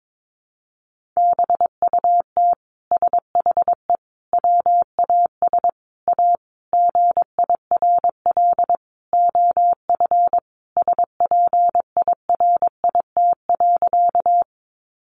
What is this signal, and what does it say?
Morse code, heard as a keyed tone, about 22 wpm, 700 Hz. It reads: BUT SHE WAS A GIRL OF SPIRIT.
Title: Sentences from Top 1400 Words 22wpm